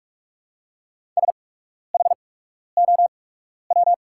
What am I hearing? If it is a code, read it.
Morse code, heard as a keyed tone, about 44 wpm, spaced as for 12 wpm, 700 Hz. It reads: SHOW
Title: 8x Character Spacing - Top 300 Words - 45wpm